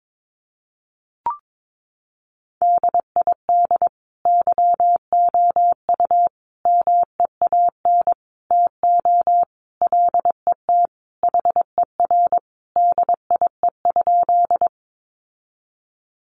Morse code, heard as a keyed tone, about 22 wpm, 700 Hz. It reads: DID YOU MEAN TO LET HER DIE?
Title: Sentences from Top 1300 Words 22wpm